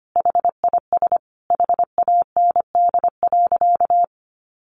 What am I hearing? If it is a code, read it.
Morse code, heard as a keyed tone, about 25 wpm, 700 Hz. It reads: HIS HAND.
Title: Sentences from Top 1400 Words 25wpm